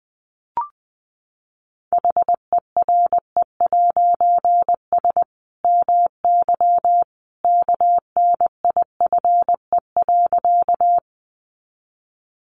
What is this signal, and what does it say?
Morse code, heard as a keyed tone, about 19 wpm, 700 Hz. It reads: HERE'S MY KNIFE.
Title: A Sound Of Thunder 20wpm